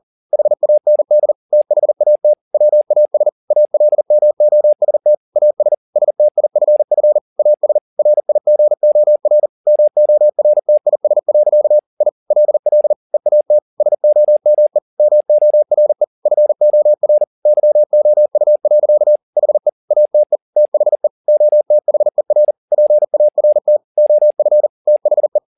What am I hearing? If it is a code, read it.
Morse code, heard as a keyed tone, about 40 wpm, 600 Hz. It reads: HAND THAT WAS ALMOST AS STIFF AS RIGOR MORTIS. I LL EAT SOME MORE FOR YOU. HE ATE THE OTHER PART OF THE